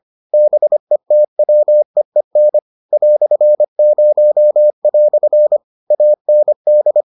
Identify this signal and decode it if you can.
Morse code, heard as a keyed tone, about 25 wpm, 600 Hz. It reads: BETWEEN "0" AND